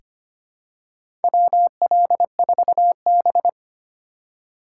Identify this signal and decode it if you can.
Morse code, heard as a keyed tone, about 25 wpm, 700 Hz. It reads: WL4B
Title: USA Call Signs - 4 Characters - 4x Character Spacing - 25wpm